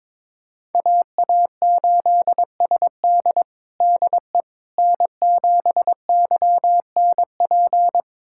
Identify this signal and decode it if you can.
Morse code, heard as a keyed tone, about 22 wpm, 700 Hz. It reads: AA8SD DE N7YNP
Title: USA Callsign DE Callsign - 5 Characters - 22wpm